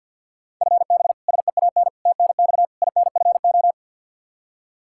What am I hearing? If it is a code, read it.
Morse code, heard as a keyed tone, about 53 wpm, 700 Hz. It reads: FB SEAN TNX INFO
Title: Practice QSO - Exchange 3 - More Info - 50wpm